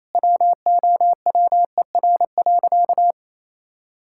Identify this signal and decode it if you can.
Morse code, heard as a keyed tone, about 28 wpm, 700 Hz. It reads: WOWER.
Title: Pangram Challenge v2 - 28wpm